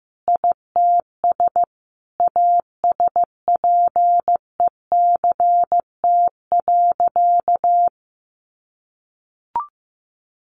Morse code, heard as a keyed tone, about 15 wpm, 700 Hz. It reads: ITS ASPECT.
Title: Sentences from Top 1300 Words 15wpm